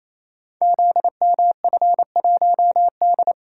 Morse code, heard as a keyed tone, about 27 wpm, 700 Hz. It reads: ZMF1D
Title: Code Groups - 5 Letter:Numbers - 28wpm